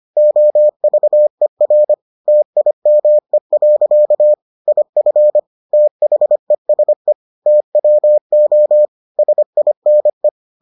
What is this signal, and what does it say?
Morse code, heard as a keyed tone, about 25 wpm, 600 Hz. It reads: OVER TIME. IF THESE TWO SINE